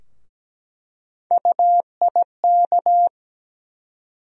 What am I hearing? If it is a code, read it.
Morse code, heard as a keyed tone, about 17 wpm, 700 Hz. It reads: UIK
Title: USA Call Sign Postfixes - 3 Letters 17wpm